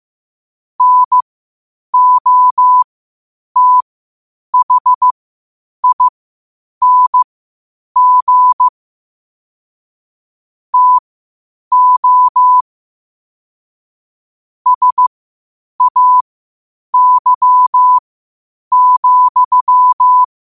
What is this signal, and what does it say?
Morse code, heard as a keyed tone, about 15 wpm, spaced as for 8 wpm, 1000 Hz. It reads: NOTHING TO SAY,
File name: qotdfiles / 8wpm